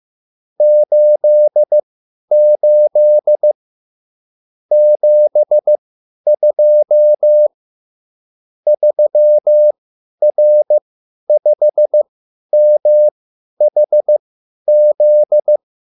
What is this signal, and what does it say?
Morse code, heard as a keyed tone, about 15 wpm, spaced as for 10 wpm, 600 Hz. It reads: 88 72 3R5MHZ